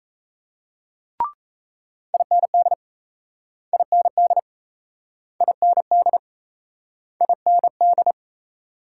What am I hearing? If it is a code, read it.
Morse code, heard as a keyed tone, about 36 wpm, 700 Hz. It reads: IND IND IND IND